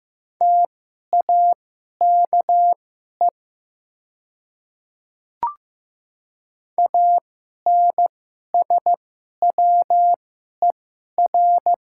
Morse code, heard as a keyed tone, about 15 wpm, spaced as for 11 wpm, 690 Hz. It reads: TAKE ANSWER
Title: 2x Character Spacing - Top 500 Words - 15wpm